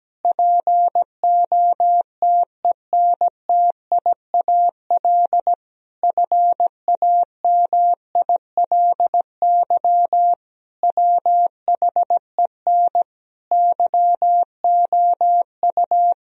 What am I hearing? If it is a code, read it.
Morse code, heard as a keyed tone, about 17 wpm, 700 Hz. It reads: POTENTIAL FAMILY WHEN YOU
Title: Sentences from Top 1000 Words - Review Part 4 - 17wpm